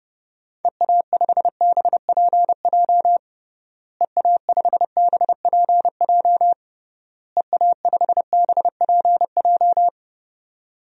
Morse code, heard as a keyed tone, about 32 wpm, 700 Hz. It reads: EA5BPJ EA5BPJ EA5BPJ
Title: DXCC-100 Call Signs - 6 Characters - Repeated 3x - 30wpm